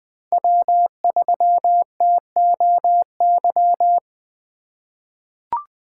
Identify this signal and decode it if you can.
Morse code, heard as a keyed tone, about 20 wpm, 700 Hz. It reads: W3TOY
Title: USA Call Signs - 5 Characters - 3x Character Spacing - 20wpm